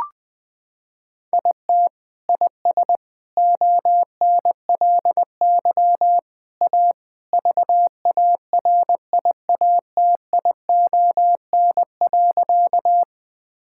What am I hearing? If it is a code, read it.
Morse code, heard as a keyed tone, about 20 wpm, 700 Hz. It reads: IT IS ONLY A VARIATION.
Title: Sentences from Top 1600 Words 20wpm